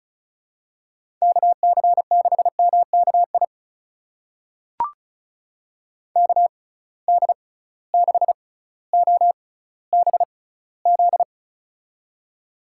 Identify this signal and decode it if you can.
Morse code, heard as a keyed tone, about 36 wpm, spaced as for 18 wpm, 700 Hz. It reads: KC6MKI KD6OBZ
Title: USA Call Signs - 6x Character Spacing - 35wpm